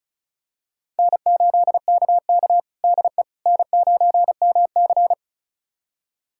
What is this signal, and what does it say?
Morse code, heard as a keyed tone, about 36 wpm, 700 Hz. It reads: N8KK DE N9MC
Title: USA Callsign DE Callsign - 4 Characters - 35wpm